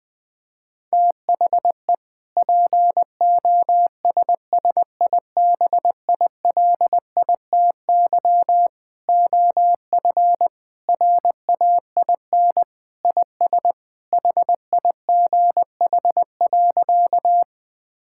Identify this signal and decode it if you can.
Morse code, heard as a keyed tone, about 21 wpm, 700 Hz. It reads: THE POSSIBILITY OF RAIN IS HIGH.
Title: Sentences with 3-Word Phrases - Spoken Beginning - 20wpm